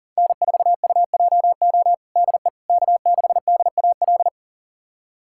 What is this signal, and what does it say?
Morse code, heard as a keyed tone, about 40 wpm, 700 Hz. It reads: N4UJO DE K6DAL
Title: USA Callsign DE Callsign - 5 Characters - 40wpm